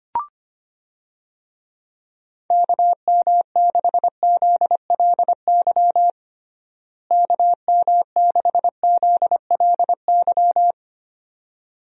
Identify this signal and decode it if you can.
Morse code, heard as a keyed tone, about 25 wpm, 700 Hz. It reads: KM6ZLY KM6ZLY
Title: USA Call Signs - 6 Characters - Repeated 2x - 25wpm